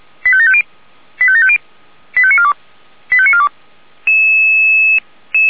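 Five-tone selective call (ZVEI), heard as quick runs of five tones, tones of about 70 ms. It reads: 85750 85750 85942 85942
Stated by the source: ZVEI five-tone selective call received on SDR.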